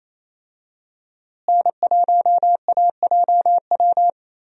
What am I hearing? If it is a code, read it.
Morse code, heard as a keyed tone, about 28 wpm, 700 Hz. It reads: N1AJW
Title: USA Call Signs - 5 Characters - Repeated 2x - 28wpm